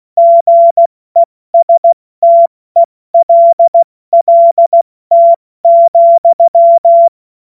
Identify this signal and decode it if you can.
Morse code, heard as a keyed tone, about 16 wpm, 680 Hz. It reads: GESTELLT,
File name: Morsekurs / Witze / Witz 21. Dezember 2021, 70 F80